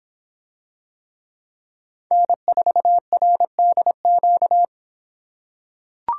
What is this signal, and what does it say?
Morse code, heard as a keyed tone, about 25 wpm, 700 Hz. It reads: N4RDQ